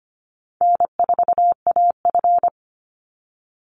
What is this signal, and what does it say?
Morse code, heard as a keyed tone, about 25 wpm, 700 Hz. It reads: N4AF
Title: Easy Call Signs Repeated 3x - 25wpm